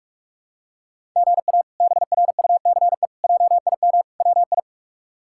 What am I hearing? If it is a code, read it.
Morse code, heard as a keyed tone, about 46 wpm, 700 Hz. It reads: GA BRUCE JIM WI
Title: K1USN SST QSO Practice - 45wpm